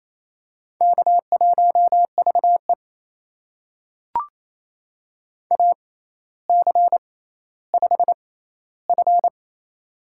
Morse code, heard as a keyed tone, about 28 wpm, spaced as for 14 wpm, 700 Hz. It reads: K1VE AC5F
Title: USA Call Signs - 4 Characters - 6x Character Spacing - 28wpm